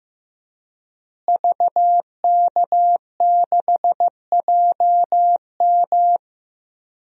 Morse code, heard as a keyed tone, about 15 wpm, 700 Hz. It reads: VK6JM
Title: DXCC-100 Call Signs - 5 Characters - Repeated 2x - 15wpm